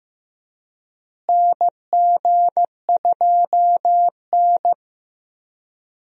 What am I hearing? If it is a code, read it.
Morse code, heard as a keyed tone, about 15 wpm, 700 Hz. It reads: NG2N
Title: USA Call Signs - 4 Characters - Repeated 2x - 15wpm